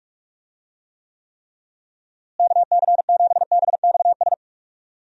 Morse code, heard as a keyed tone, about 43 wpm, 700 Hz. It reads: KC7BXS
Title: USA Call Signs - 6 Characters - Repeated 3x - 45wpm